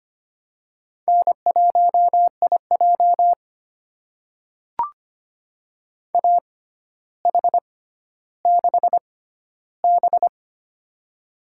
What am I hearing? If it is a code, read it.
Morse code, heard as a keyed tone, about 25 wpm, spaced as for 13 wpm, 700 Hz. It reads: N1IJ AH6B
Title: USA Call Signs - 4 Characters - 6x Character Spacing - 25wpm